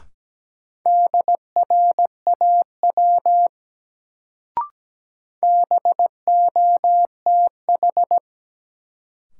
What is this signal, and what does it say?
Morse code, heard as a keyed tone, about 17 wpm, 700 Hz. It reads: DRAW BOTH
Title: Sentences from Top 300 Words - New Word Drill - 17wpm